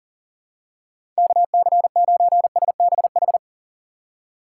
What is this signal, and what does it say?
Morse code, heard as a keyed tone, about 40 wpm, 700 Hz. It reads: KC9SBH
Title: USA Call Signs - 6 Characters - 6x Character Spacing - 40wpm